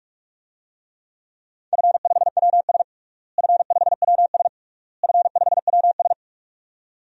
Morse code, heard as a keyed tone, about 46 wpm, 700 Hz. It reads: F5WS F5WS F5WS
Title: DXCC-100 Call Signs - 4 Characters - Repeated 3x - 45wpm